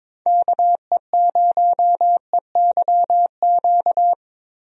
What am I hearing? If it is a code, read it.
Morse code, heard as a keyed tone, about 21 wpm, 700 Hz. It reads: KE0EYQ
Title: USA Call Signs - 6 Characters - Repeated 2x - 22wpm